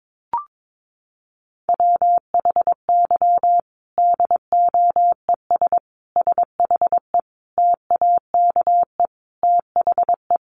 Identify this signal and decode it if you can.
Morse code, heard as a keyed tone, about 22 wpm, 700 Hz. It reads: WHY DOES SHE TAKE THE
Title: Sentences from Top 1300 Words 22wpm